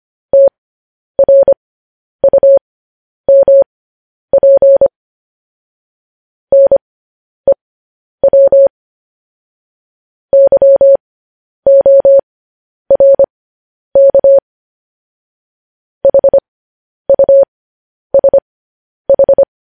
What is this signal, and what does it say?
Morse code, heard as a keyed tone, about 25 wpm, spaced as for 10 wpm, 550 Hz. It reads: TRUMP NEW YORK HUSH